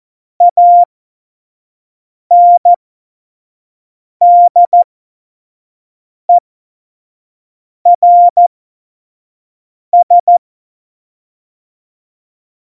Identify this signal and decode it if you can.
Morse code, heard as a keyed tone, about 14 wpm, spaced as for 5 wpm, 700 Hz. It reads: ANDERS
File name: Morsekurs / Witze / Witz 04. Juni 2019, 25 F70